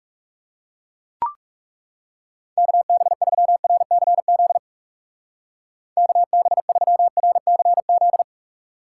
Morse code, heard as a keyed tone, about 40 wpm, 700 Hz. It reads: KB3RCZ KB3RCZ